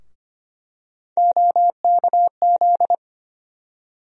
Morse code, heard as a keyed tone, about 25 wpm, 700 Hz. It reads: OKZ
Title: USA Call Sign Postfixes - 3 Letters 25wpm